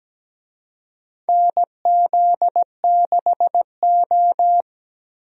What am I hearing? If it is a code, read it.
Morse code, heard as a keyed tone, about 17 wpm, 700 Hz. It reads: NZ6O
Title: USA Call Signs - 4 Characters - Repeated 3x - 17wpm